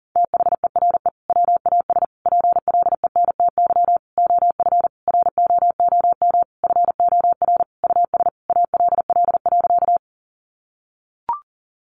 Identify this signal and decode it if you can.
Morse code, heard as a keyed tone, about 40 wpm, 700 Hz. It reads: THERE WAS PLENTY OF ROOM FOR US ALL.
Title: Sentences from Top 1300 Words 40wpm